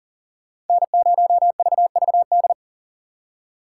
Morse code, heard as a keyed tone, about 52 wpm, 700 Hz. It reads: N0VVD